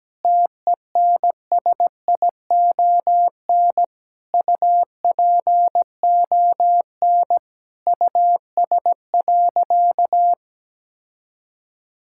Morse code, heard as a keyed tone, about 17 wpm, 700 Hz. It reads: TENSION UPON US.
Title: Sentences from Top 1600 Words 17wpm